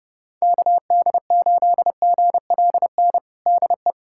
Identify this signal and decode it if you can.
Morse code, heard as a keyed tone, about 29 wpm, 700 Hz. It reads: KD8GLN DE
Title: USA Callsign DE Callsign 30wpm